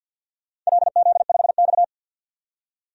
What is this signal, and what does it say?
Morse code, heard as a keyed tone, about 47 wpm, 700 Hz. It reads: LZ5X
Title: DXCC-100 Call Signs - 4 Characters - 50wpm